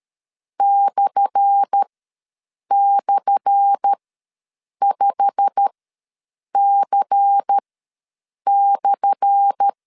Morse code, heard as a keyed tone, about 13 wpm, spaced as for 7 wpm, 800 Hz. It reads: //5C/
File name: LEZIONI / 10 SIGNS / 10A SLASH ws 12wpm 3443